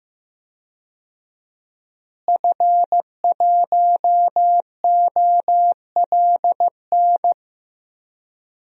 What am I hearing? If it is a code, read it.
Morse code, heard as a keyed tone, about 15 wpm, 700 Hz. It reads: F1OLN
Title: DXCC-100 Call Signs - 5 Characters - Repeated 3x - 15wpm